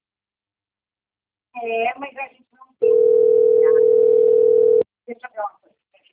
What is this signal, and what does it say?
Ringback tone (North America), 440 + 480 Hz, with speech.
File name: rg-1600-unknown- - - .10876